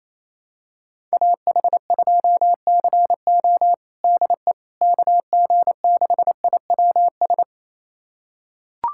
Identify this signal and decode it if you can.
Morse code, heard as a keyed tone, about 28 wpm, 700 Hz. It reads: AH2CO DE KG6IWS
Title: USA Callsign DE Callsign 28wpm